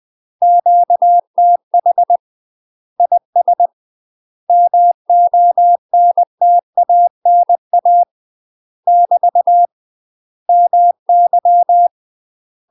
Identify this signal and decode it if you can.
Morse code, heard as a keyed tone, about 20 wpm, 700 Hz. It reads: QTH IS MONTANA <BT> MY
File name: UK MASTER QSOs TRAINING / 20WPM / UK MASTER QSOs 0044